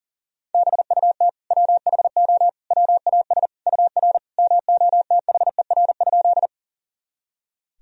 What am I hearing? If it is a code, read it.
Morse code, heard as a keyed tone, about 40 wpm, 700 Hz. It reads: BUT WHO WAS UR MOTHER?
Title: Sentences from Top 200 Words and Abbreviations 40wpm